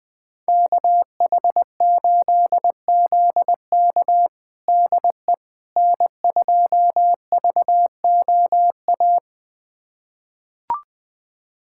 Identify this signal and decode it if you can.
Morse code, heard as a keyed tone, about 20 wpm, 700 Hz. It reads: KH8ZK DE N2VOA
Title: USA Callsign DE Callsign - 5 Characters - 20wpm